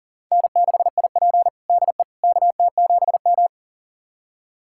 Morse code, heard as a keyed tone, about 40 wpm, 700 Hz. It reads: N6IP DE KT7M
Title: USA Callsign DE Callsign - 4 Characters - 40wpm